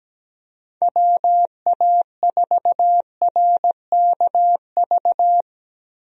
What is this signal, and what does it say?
Morse code, heard as a keyed tone, about 17 wpm, 700 Hz. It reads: WA4RKV
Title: USA Call Signs - 6 Characters - Repeated 3x - 17wpm